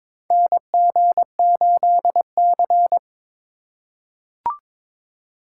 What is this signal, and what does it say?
Morse code, heard as a keyed tone, about 22 wpm, 700 Hz. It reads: NG8C
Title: USA Call Signs - 4 Characters - 3x Character Spacing - 22wpm